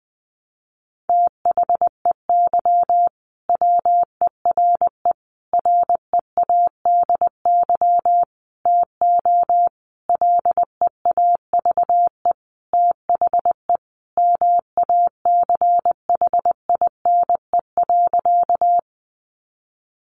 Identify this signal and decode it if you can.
Morse code, heard as a keyed tone, about 20 wpm, 700 Hz. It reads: THEY WERE READY TO LEAVE THE MACHINE.
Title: A Sound Of Thunder 20wpm